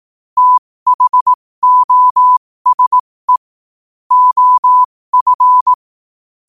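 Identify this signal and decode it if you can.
Morse code, heard as a keyed tone, about 18 wpm, spaced as for 15 wpm, 1000 Hz. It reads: THOSE OF
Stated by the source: Morse quote-of-the-day practice at 15 WPM from Morse Resource.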